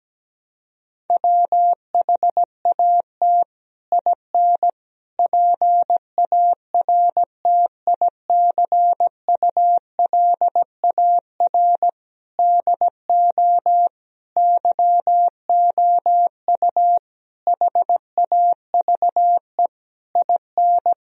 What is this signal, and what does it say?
Morse code, heard as a keyed tone, about 17 wpm, 700 Hz. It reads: WHAT IN PARTICULAR DO YOU HAVE IN
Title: Sentences from Top 1000 Words - Review Part 4 - 17wpm